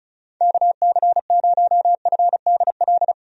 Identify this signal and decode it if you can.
Morse code, heard as a keyed tone, about 36 wpm, 700 Hz. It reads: KC0FDL
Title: USA Call Signs - 6 Characters - 35wpm